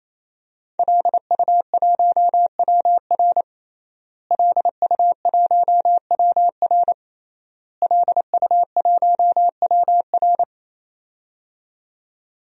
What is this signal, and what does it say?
Morse code, heard as a keyed tone, about 28 wpm, 700 Hz. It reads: LU1WR LU1WR LU1WR
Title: DXCC-100 Call Signs - 5 Characters - Repeated 3x - 28wpm